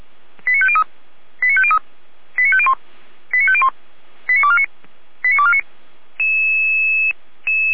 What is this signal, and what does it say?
Five-tone selective call (ZVEI), heard as quick runs of five tones, tones of about 70 ms. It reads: 89503 89503 89501 89501 89249 89249